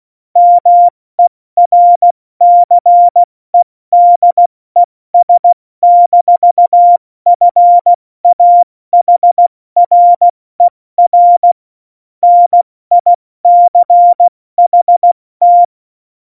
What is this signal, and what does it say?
Morse code, heard as a keyed tone, about 16 wpm, spaced as for 14 wpm, 700 Hz. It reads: MERCEDES-FAHRER NICHT